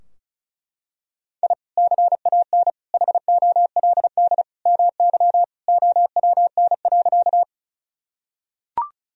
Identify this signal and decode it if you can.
Morse code, heard as a keyed tone, about 36 wpm, 700 Hz. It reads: I CAN HOLD MY OWN.
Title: Sentences from Top 1000 Words - Review Part 2 - 35wpm